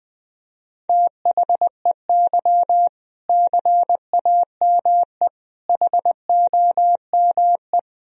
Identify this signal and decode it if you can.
Morse code, heard as a keyed tone, about 21 wpm, 700 Hz. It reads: THEY CAME HOME
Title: Sentences from Top 1000 Words - Review Part 3 - 20wpm